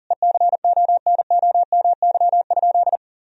Morse code, heard as a keyed tone, about 40 wpm, 700 Hz. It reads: ECONOMY?
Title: Sentences from Top 1000 Words - Review Part 4 - 40wpm